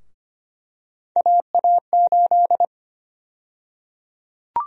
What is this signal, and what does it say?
Morse code, heard as a keyed tone, about 25 wpm, 700 Hz. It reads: AA8
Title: USA Call Sign Prefixes - Repeated 2x - 25wpm